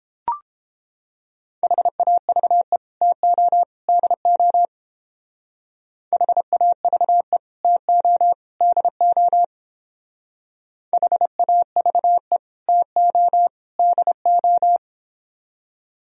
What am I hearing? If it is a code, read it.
Morse code, heard as a keyed tone, about 29 wpm, 700 Hz. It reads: HAVE TO DO HAVE TO DO HAVE TO DO